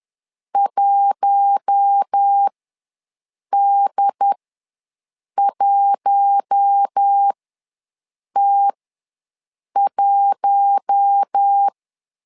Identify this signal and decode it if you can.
Morse code, heard as a keyed tone, about 10 wpm, spaced as for 6 wpm, 800 Hz. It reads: 1D1T1
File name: LEZIONI / 4 Q1 ws 10wpm 2538